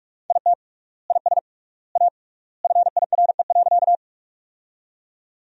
Morse code, heard as a keyed tone, about 44 wpm, 700 Hz. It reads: IT IS A FIRE.
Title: 3x Word-Spacing - Sentences from Top 400 Words - 45wpm